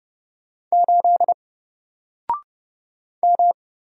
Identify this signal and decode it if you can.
Morse code, heard as a keyed tone, about 31 wpm, 700 Hz. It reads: 8 M
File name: Single-Letter-Number-30wpm